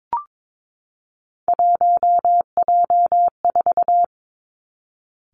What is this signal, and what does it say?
Morse code, heard as a keyed tone, about 22 wpm, 700 Hz. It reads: 1J4
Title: Three Challenging Letter-Number Combinations 22wpm